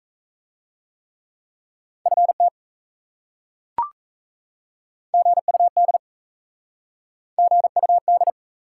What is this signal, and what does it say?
Morse code, heard as a keyed tone, about 37 wpm, 700 Hz. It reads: FT GUD GUD